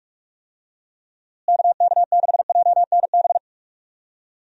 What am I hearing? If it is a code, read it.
Morse code, heard as a keyed tone, about 46 wpm, 700 Hz. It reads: KK6JNB
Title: USA Call Signs - 6 Characters - Repeated 3x - 45wpm